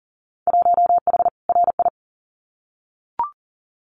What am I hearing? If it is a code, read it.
Morse code, heard as a keyed tone, about 40 wpm, 700 Hz. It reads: 1H RI
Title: Winter Field Day Exchanges 40wpm